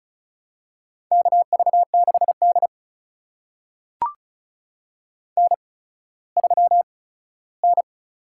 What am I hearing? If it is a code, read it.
Morse code, heard as a keyed tone, about 36 wpm, spaced as for 17 wpm, 700 Hz. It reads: KV6D N3N
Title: USA Call Signs - 4 Characters - 8x Character Spacing - 35wpm